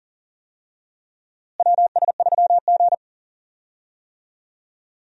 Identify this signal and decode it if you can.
Morse code, heard as a keyed tone, about 40 wpm, 700 Hz. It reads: WS3G
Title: USA Call Signs - 4 Characters - Repeated 3x - 40wpm